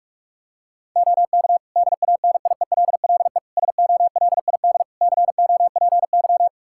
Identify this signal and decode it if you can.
Morse code, heard as a keyed tone, about 46 wpm, 700 Hz. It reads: OK DANIELLE SOLID COPY
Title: Practice QSO - Exchange 3 - More Info - 45wpm